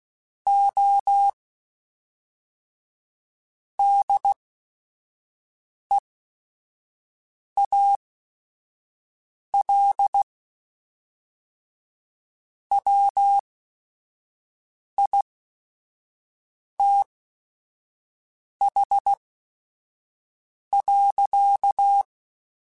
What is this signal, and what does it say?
Morse code, heard as a keyed tone, about 16 wpm, spaced as for 5 wpm, 800 Hz. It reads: O DEAL WITH.
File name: Technical Subjects / Technical Subjects-mp3s / AND Gate @5